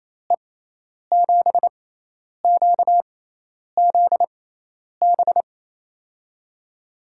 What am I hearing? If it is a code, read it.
Morse code, heard as a keyed tone, about 27 wpm, spaced as for 10 wpm, 700 Hz. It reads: E7QZB
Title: USA Call Signs - 6 Characters - 6x Character Spacing - 28wpm